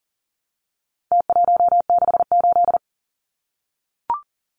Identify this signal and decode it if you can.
Morse code, heard as a keyed tone, about 40 wpm, 700 Hz. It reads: T168
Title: Numbers - Contest Style - 4-Digit 40wpm